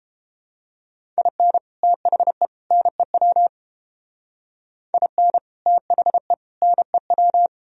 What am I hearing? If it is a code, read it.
Morse code, heard as a keyed tone, about 31 wpm, 700 Hz. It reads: IN THE NEW IN THE NEW